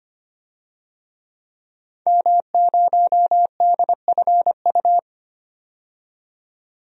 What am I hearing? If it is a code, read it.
Morse code, heard as a keyed tone, about 25 wpm, 700 Hz. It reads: M0DFU
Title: DXCC-100 Call Signs - 5 Characters - 25wpm